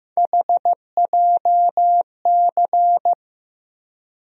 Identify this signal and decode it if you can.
Morse code, heard as a keyed tone, about 15 wpm, 700 Hz. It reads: HJC
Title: USA Call Sign Postfixes - 3 Letters 15wpm